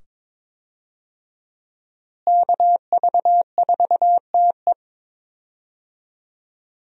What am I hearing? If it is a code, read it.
Morse code, heard as a keyed tone, about 21 wpm, 700 Hz. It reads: KV4TE
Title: USA Call Signs - 5 Characters - Repeated 2x - 22wpm